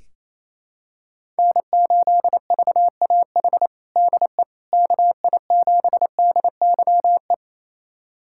Morse code, heard as a keyed tone, about 28 wpm, 700 Hz. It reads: N8VAH DE KI7DYE
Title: USA Callsign DE Callsign 28wpm